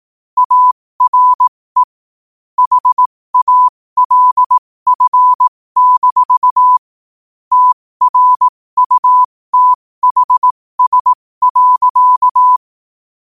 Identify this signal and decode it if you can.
Morse code, heard as a keyed tone, about 18 wpm, spaced as for 15 wpm, 1000 Hz. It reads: ARE HALF- TRUTHS.
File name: qotdfiles / 15wpm